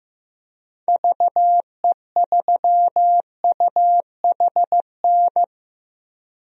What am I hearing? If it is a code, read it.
Morse code, heard as a keyed tone, about 15 wpm, 700 Hz. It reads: VE3UHN
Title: DXCC-100 Call Signs - 6 Characters - Repeated 2x - 15wpm